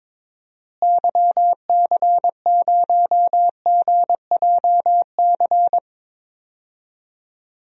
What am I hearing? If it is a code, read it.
Morse code, heard as a keyed tone, about 22 wpm, 700 Hz. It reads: YC0GJC
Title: DXCC-100 Call Signs - 6 Characters - Repeated 3x - 22wpm